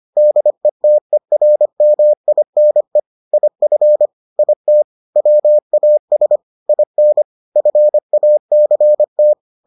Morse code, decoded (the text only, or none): DETERMINE IF IT WAS IN FACT